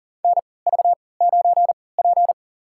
Morse code, heard as a keyed tone, about 40 wpm, spaced as for 23 wpm, 700 Hz. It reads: NV9P
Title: USA Call Signs - 3x Character Spacing - 40wpm